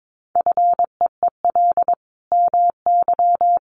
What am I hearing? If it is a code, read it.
Morse code, heard as a keyed tone, about 22 wpm, 700 Hz. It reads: FEEL MY
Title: Sentences from Top 1100 Words 22wpm